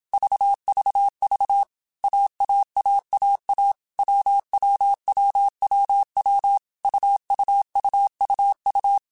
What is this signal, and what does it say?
Morse code, heard as a keyed tone, about 27 wpm, 800 Hz. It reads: VVV AAAAA WWWWW UUUUU
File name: Basic Sets / Basic Letter Groups Collective7 / AWUNDG@25